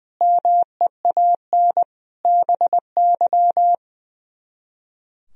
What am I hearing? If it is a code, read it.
Morse code, heard as a keyed tone, about 22 wpm, 700 Hz. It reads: MEAN BY
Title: Speed-Racing - Sets of 5 Words 20wpm